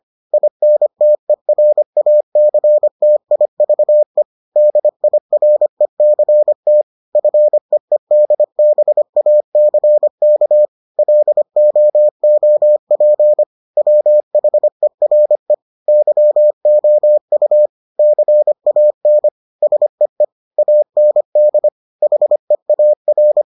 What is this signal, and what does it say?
Morse code, heard as a keyed tone, about 25 wpm, 600 Hz. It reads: INTERACTIVE DIRECT FEEDBACK LOOP WHERE YOU CAN SEE AND HEAR